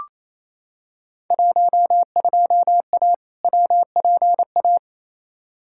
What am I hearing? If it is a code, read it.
Morse code, heard as a keyed tone, about 28 wpm, 700 Hz. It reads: 12A WPA
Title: Field Day Exchanges Encore - 28wpm